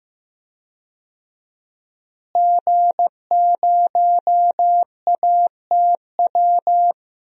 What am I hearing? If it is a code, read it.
Morse code, heard as a keyed tone, about 15 wpm, 700 Hz. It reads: G0ATW
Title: DXCC-100 Call Signs - 5 Characters - Repeated 2x - 15wpm